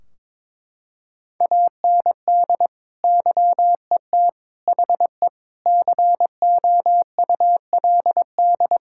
Morse code, heard as a keyed tone, about 22 wpm, 700 Hz. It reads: AND YET HE COULD
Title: Sentences from Top 1600 Words 22wpm